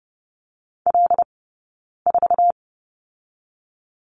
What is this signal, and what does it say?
Morse code, heard as a keyed tone, about 30 wpm, 700 Hz. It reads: L 4
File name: Lesson 20d - Number 3 - Two Characters 30wpm